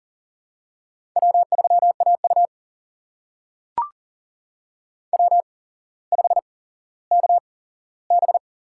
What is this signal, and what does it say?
Morse code, heard as a keyed tone, about 40 wpm, spaced as for 17 wpm, 700 Hz. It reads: W3AU W5KB
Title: USA Call Signs - 4 Characters - 8x Character Spacing - 40wpm